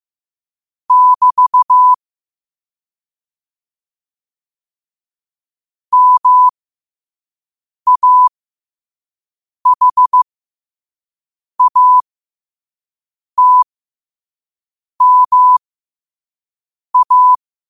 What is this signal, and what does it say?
Morse code, heard as a keyed tone, about 15 wpm, spaced as for 5 wpm, 1000 Hz. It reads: <BT> MAHATMA
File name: qotdfiles / 5wpm